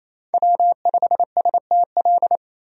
Morse code, heard as a keyed tone, about 27 wpm, 700 Hz. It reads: W5STL
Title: USA Call Signs - 5 Characters - 28wpm